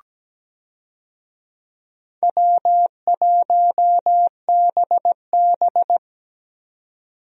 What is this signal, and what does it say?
Morse code, heard as a keyed tone, about 17 wpm, 700 Hz. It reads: W1BB
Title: USA Call Signs - 4 Characters - 17wpm